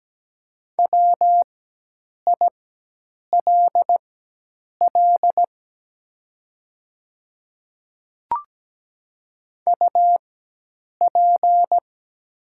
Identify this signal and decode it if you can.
Morse code, heard as a keyed tone, about 17 wpm, spaced as for 8 wpm, 690 Hz. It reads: WILL UP
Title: 4x Character Spacing - Top 200 Words - 17wpm